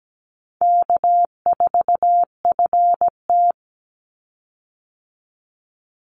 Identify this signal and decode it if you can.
Morse code, heard as a keyed tone, about 17 wpm, 700 Hz. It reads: K4FT
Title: Call Signs - 17wpm